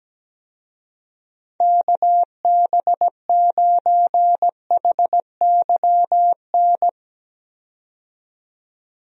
Morse code, heard as a keyed tone, about 17 wpm, 700 Hz. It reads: KB9HYN
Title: USA Call Signs - 6 Characters - 17wpm